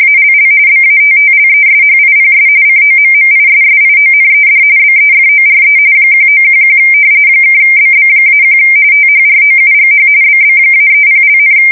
RTTY, about 45.45 baud. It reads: ALL BATTLESHIPS COMMISSIONED BEFORE 1911 (BB-1 THROUGH BB-30),
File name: fsk170 / battleshipnj